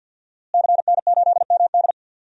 Morse code, heard as a keyed tone, about 52 wpm, 700 Hz. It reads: CN8MD
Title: DXCC-100 Call Signs - 5 Characters - Repeated 2x - 50wpm